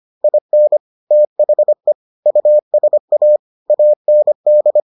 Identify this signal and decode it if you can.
Morse code, heard as a keyed tone, about 25 wpm, 600 Hz. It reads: IN THE USA AND